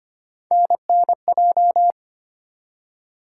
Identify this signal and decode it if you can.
Morse code, heard as a keyed tone, about 25 wpm, 700 Hz. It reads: NNJ
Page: morse-code-ninja-practice-25wpm